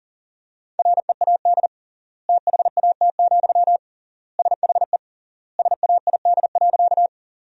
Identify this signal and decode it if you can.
Morse code, heard as a keyed tone, about 40 wpm, 700 Hz. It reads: READ THAT, SHE SAID.
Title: 3x Word-Spacing - Sentences from Top 200 Words - 40wpm